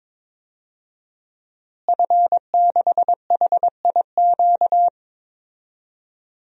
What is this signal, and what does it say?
Morse code, heard as a keyed tone, about 22 wpm, 700 Hz. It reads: F6HIQ
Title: DXCC-100 Call Signs - 5 Characters - Repeated 2x - 22wpm